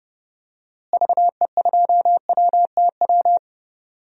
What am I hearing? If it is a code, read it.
Morse code, heard as a keyed tone, about 30 wpm, 700 Hz. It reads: VE2WTW
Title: DXCC-100 Call Signs - 6 Characters - Repeated 2x - 30wpm